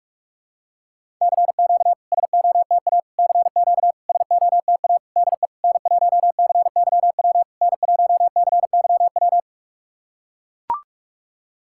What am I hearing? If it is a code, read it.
Morse code, heard as a keyed tone, about 44 wpm, 690 Hz. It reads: CQ SOTA CQ SOTA DE N1CYW N1CYW
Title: SOTA QSO Practice 45wpm